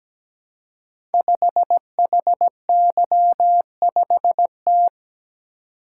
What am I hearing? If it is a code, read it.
Morse code, heard as a keyed tone, about 17 wpm, 700 Hz. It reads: 5HY5T
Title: Code Groups - 5 Letter:Numbers - 17wpm